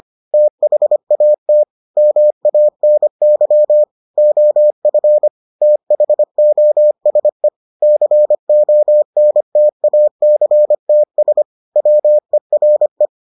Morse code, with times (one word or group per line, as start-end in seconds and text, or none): 0.3-1.6 THAT
2.0-3.8 MANY
4.2-5.3 OF
5.6-7.5 THOSE
7.8-11.4 CONTACTS
11.8-13.1 WERE